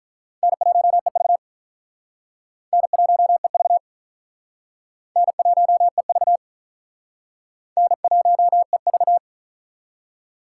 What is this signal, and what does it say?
Morse code, heard as a keyed tone, about 46 wpm, 700 Hz. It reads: N1EV N1EV N1EV N1EV